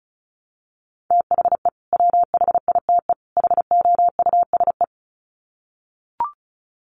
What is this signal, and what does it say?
Morse code, heard as a keyed tone, about 34 wpm, 700 Hz. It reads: THE WHITE HOUSE
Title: Sets of 3 Words 35wpm